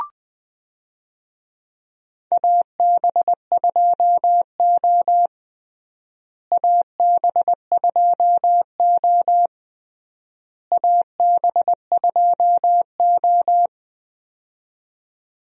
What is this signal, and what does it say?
Morse code, heard as a keyed tone, about 20 wpm, 700 Hz. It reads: AB2O AB2O AB2O
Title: USA Call Signs - 4 Characters - Repeated 3x - 20wpm